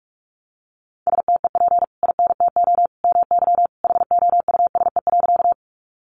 Morse code, heard as a keyed tone, about 44 wpm, 700 Hz. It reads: STEP INTO MY HOUSE.
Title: Sentences from Top 600 Words 45wpm